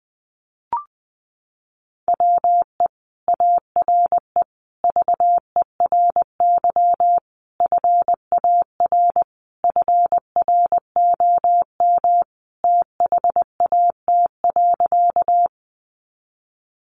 Morse code, heard as a keyed tone, about 20 wpm, 700 Hz. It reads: WE ARE VERY FAR FROM THAT.
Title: Sentences from Top 300 Words 20wpm